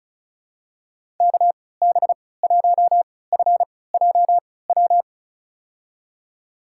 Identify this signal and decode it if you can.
Morse code, heard as a keyed tone, about 34 wpm, spaced as for 20 wpm, 700 Hz. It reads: KB1FJW
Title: USA Call Signs - 3x Character Spacing - 35wpm